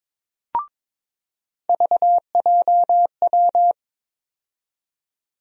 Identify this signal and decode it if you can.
Morse code, heard as a keyed tone, about 22 wpm, 700 Hz. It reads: VJW
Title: USA Call Sign Postfixes - 3 Letters 22wpm